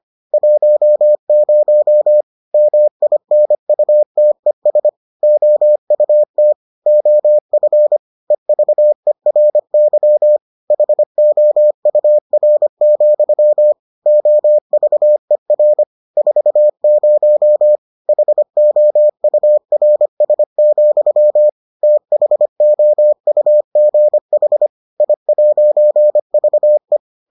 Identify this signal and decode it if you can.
Morse code, heard as a keyed tone, about 25 wpm, 600 Hz. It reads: 10 MINUTES OUT OF EVERY HOUR, OVER 40 HOURS, THOUGH I'VE